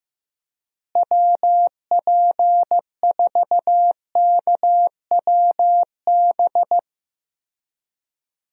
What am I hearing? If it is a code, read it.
Morse code, heard as a keyed tone, about 15 wpm, 700 Hz. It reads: WP4KWB
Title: USA Call Signs - 6 Characters - Repeated 3x - 15wpm